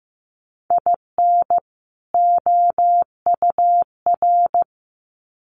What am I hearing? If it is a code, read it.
Morse code, heard as a keyed tone, about 15 wpm, 700 Hz. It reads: IN OUR
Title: Sets of 2 Words 15wpm